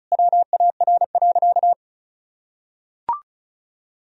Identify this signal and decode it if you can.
Morse code, heard as a keyed tone, about 36 wpm, 700 Hz. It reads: WAR.
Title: Sentences from Top 1000 Words - Review Part 4 - 35wpm